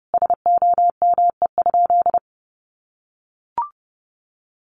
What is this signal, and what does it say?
Morse code, heard as a keyed tone, about 30 wpm, 700 Hz. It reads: SOME?
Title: Sentences from Top 100 Words 30wpm